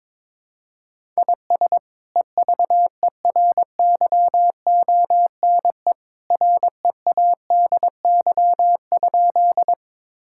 Morse code, heard as a keyed tone, about 21 wpm, 700 Hz. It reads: IS EVERYONE READY?
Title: Sentences from Top 1000 Words - Review Part 5 - 22wpm